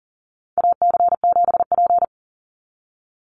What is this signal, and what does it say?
Morse code, heard as a keyed tone, about 40 wpm, 700 Hz. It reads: AC7P
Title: NA SOTA Chaser Callsigns 40wpm